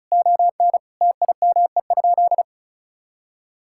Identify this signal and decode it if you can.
Morse code, heard as a keyed tone, about 36 wpm, 700 Hz. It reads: ON TIME?
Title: Sentences from Top 1000 Words - Review Part 6 - 35wpm